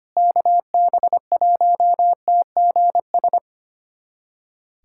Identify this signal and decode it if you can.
Morse code, heard as a keyed tone, about 25 wpm, 700 Hz. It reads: KB1TGS
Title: USA Callsign DE Callsign - 6 Characters - 25wpm